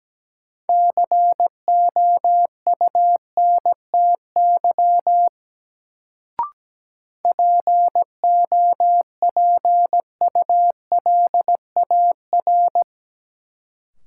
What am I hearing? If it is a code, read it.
Morse code, heard as a keyed tone, about 17 wpm, 700 Hz. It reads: COUNTY POPULAR
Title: Sentences from Top 1200 Words - New Word Drill Alt - 17wpm